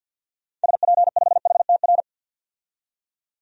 Morse code, heard as a keyed tone, about 50 wpm, 700 Hz. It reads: SP5HTR
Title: DXCC-100 Call Signs - 6 Characters - 50wpm